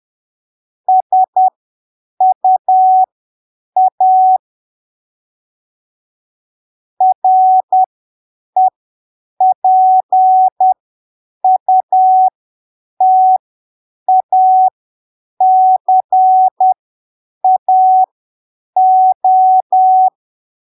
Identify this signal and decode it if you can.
Morse code, heard as a keyed tone, about 10 wpm, spaced as for 7 wpm, 750 Hz. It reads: SUA REPUTACAO
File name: arquivos / texto / texto 28 10ppm